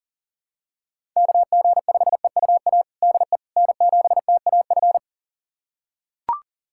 Morse code, heard as a keyed tone, about 40 wpm, 700 Hz. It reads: KG5EUA DE N7TAF
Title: USA Callsign DE Callsign 40wpm